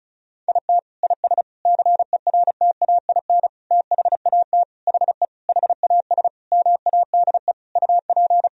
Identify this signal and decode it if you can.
Morse code, heard as a keyed tone, about 34 wpm, 700 Hz. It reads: IT IS CERTAIN THAT HE HAS MADE UP
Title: Sentences from Top 1000 Words - Review Part 2 - 35wpm